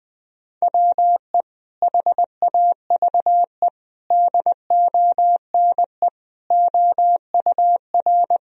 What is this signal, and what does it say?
Morse code, heard as a keyed tone, about 20 wpm, 700 Hz. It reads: WE HAVE DONE OUR
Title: Sentences from Top 1000 Words - Review Part 1 - 20wpm